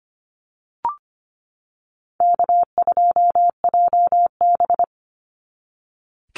Morse code, heard as a keyed tone, about 25 wpm, 700 Hz. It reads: K2JB
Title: NA SOTA Chaser Callsigns 25wpm